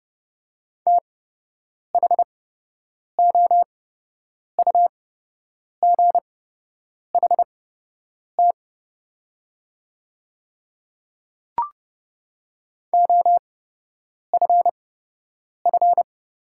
Morse code, heard as a keyed tone, about 30 wpm, spaced as for 8 wpm, 700 Hz. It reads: THOUGHT OFF
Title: 8x Character Spacing - Top 300 Words - 30wpm